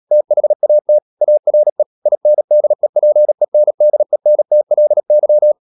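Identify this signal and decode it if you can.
Morse code, heard as a keyed tone, about 37 wpm, 600 Hz. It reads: THAT ARE INDEPENDENTLY
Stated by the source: Morse code practice at 35–39 wpm sending English text of Ken Follett's novel Never, from CW med Gnister.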